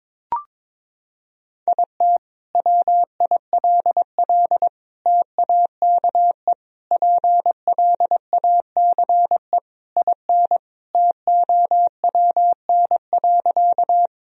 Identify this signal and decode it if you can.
Morse code, heard as a keyed tone, about 22 wpm, 700 Hz. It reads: IT WILL TAKE PLACE IN TOWN.
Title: Sentences from Top 1000 Words - Review Part 2 - 22wpm